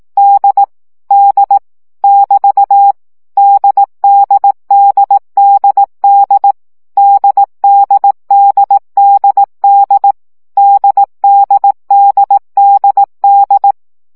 Morse code, heard as a keyed tone, about 18 wpm, 800 Hz. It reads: D D <BT> DDDDD DDDDD DDDDD
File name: CW gyakorlas d 15 baud